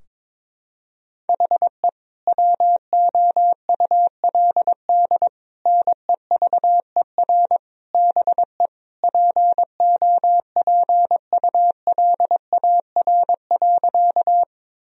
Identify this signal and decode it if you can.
Morse code, heard as a keyed tone, about 21 wpm, 700 Hz. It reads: HE WOULD NEVER BE POPULAR.
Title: Sentences from Top 1000 Words - Review Part 3 - 22wpm